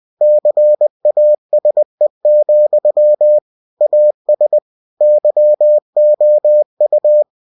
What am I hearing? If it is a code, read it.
Morse code, heard as a keyed tone, about 20 wpm, 600 Hz. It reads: CASE, AS YOU
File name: cw-books / winnie-the-pooh / winnie-20wpm-page-0006